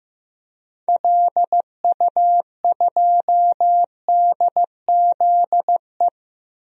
Morse code, heard as a keyed tone, about 15 wpm, 700 Hz. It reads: LU2DZE
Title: DXCC-100 Call Signs - 6 Characters - 15wpm